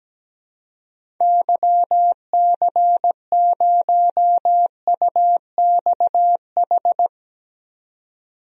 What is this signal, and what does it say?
Morse code, heard as a keyed tone, about 17 wpm, 700 Hz. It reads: YC0UXH
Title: DXCC-100 Call Signs - 6 Characters - 17wpm